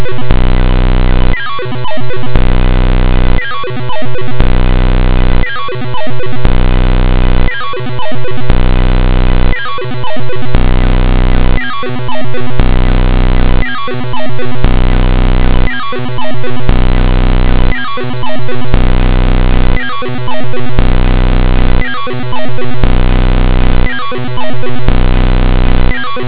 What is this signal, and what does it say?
There is no code.